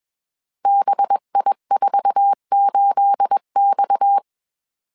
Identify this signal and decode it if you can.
Morse code, heard as a keyed tone, about 21 wpm, 800 Hz. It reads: BI48X
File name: CW / DET / 1 9 1500C S C 20wpm 2310